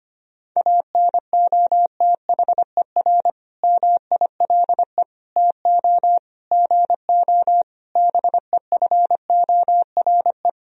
Morse code, heard as a keyed tone, about 25 wpm, 700 Hz. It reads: ANOTHER MILE TO GO BEFORE
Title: Sentences from Top 1000 Words - Review Part 1 - 25wpm